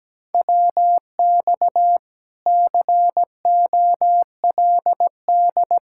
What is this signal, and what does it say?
Morse code, heard as a keyed tone, about 17 wpm, 700 Hz. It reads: WX COLD
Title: Speed-Racing Common QSO Phrases Encore - 17wpm